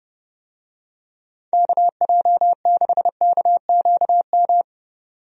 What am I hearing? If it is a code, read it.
Morse code, heard as a keyed tone, about 30 wpm, 700 Hz. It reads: KJ6KQM
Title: USA Call Signs - 6 Characters - 30wpm